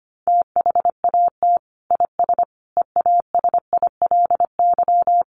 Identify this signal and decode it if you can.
Morse code, heard as a keyed tone, about 25 wpm, 700 Hz. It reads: THAT IS EASILY
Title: Sentences from Top 900 Words 25wpm